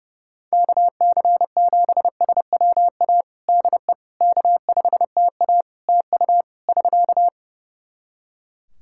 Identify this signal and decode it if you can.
Morse code, heard as a keyed tone, about 30 wpm, 700 Hz. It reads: KC7SWA DE K5TA TU <SK>
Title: Practice QSO - Exchange 4 - Ending - 30wpm